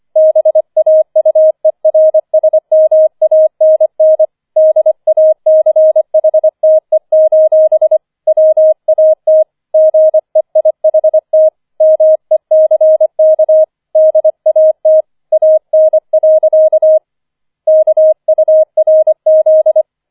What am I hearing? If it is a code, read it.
Morse code, heard as a keyed tone, about 25 wpm, 630 Hz. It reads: BAUERSMANN DACHTE: WAT GEIHT MECK DAT AN. KURZ